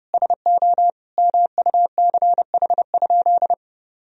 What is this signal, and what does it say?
Morse code, heard as a keyed tone, about 30 wpm, 700 Hz. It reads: SO MUCH?
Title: Sentences from Top 1000 Words - Review Part 3 - 30wpm